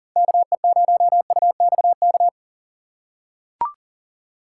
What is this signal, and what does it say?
Morse code, heard as a keyed tone, about 40 wpm, 700 Hz. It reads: KE0UXK
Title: USA Call Signs - 6 Characters - 4x Character Spacing - 40wpm